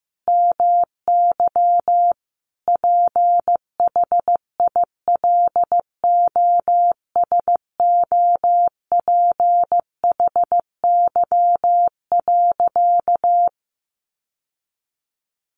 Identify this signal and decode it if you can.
Morse code, heard as a keyed tone, about 14 wpm, 700 Hz. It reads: MY PHILOSOPHY.
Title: Sentences from Top 900 Words 15wpm